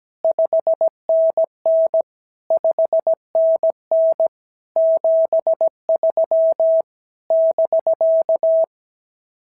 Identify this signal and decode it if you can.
Morse code, heard as a keyed tone, about 16 wpm, 650 Hz. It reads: 5NN 5NN 73 <BK>
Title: SOTA QSO Practice 17wpm